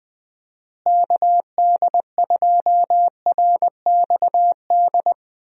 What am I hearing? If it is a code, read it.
Morse code, heard as a keyed tone, about 21 wpm, 700 Hz. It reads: KD2RXD
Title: USA Call Signs - 6 Characters - Repeated 2x - 20wpm